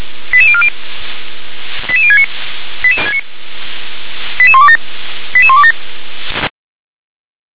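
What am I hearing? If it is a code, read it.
Five-tone selective call (ZVEI), heard as quick runs of five tones, tones of about 70 ms. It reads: 88040 88069 88127 88127
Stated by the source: ZVEI five-tone call received with an SDR.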